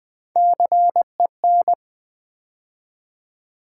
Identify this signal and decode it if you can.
Morse code, heard as a keyed tone, about 20 wpm, 700 Hz. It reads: CEN
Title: Build A Word - 3 Letters At A Time - 6-Letter Words - 20wpm